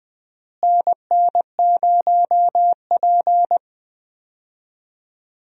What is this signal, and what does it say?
Morse code, heard as a keyed tone, about 21 wpm, 700 Hz. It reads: NN0P
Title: USA Call Signs - 4 Characters - Repeated 3x - 20wpm